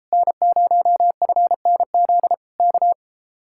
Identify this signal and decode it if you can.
Morse code, heard as a keyed tone, about 33 wpm, 700 Hz. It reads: N0FNZ K